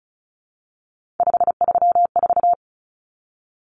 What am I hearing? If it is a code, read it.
Morse code, heard as a keyed tone, about 34 wpm, 700 Hz. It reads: 534
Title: Common Cut Numbers - 3-Digit 35wpm